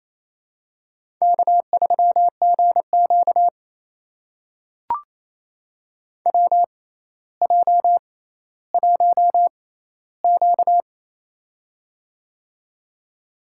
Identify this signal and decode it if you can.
Morse code, heard as a keyed tone, about 28 wpm, spaced as for 14 wpm, 700 Hz. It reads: K3GQ WJ1Q
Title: USA Call Signs - 4 Characters - 6x Character Spacing - 28wpm